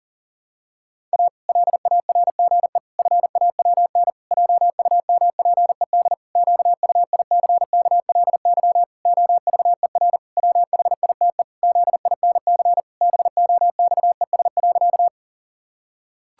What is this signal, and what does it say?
Morse code, heard as a keyed tone, about 40 wpm, 700 Hz. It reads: A LARGE FAWN JUMPED QUICKLY OVER WHITE ZINC BOXES.
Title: Pangram Challenge v2 - 40wpm